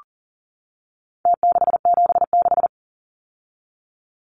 Morse code, heard as a keyed tone, about 40 wpm, 700 Hz. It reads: T676
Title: Numbers - Contest Style - 4-Digit 40wpm